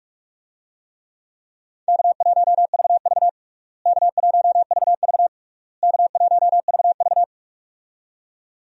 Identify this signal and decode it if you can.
Morse code, heard as a keyed tone, about 46 wpm, 700 Hz. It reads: K1VV K1VV K1VV
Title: USA Call Signs - 4 Characters - Repeated 3x - 45wpm